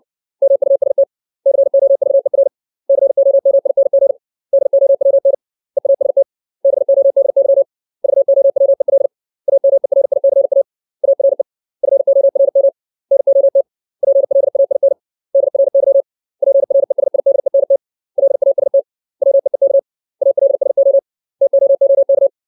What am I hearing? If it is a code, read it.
Morse code, heard as a keyed tone, about 60 wpm, 550 Hz. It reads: MUST COVER COMING DOWN EAST BODY VOWEL AMERICA ARE FORM NOT PLAIN DRY PRESIDENT LAST WEEK ALSO TOOK
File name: 632 words / 632 Words 60WPM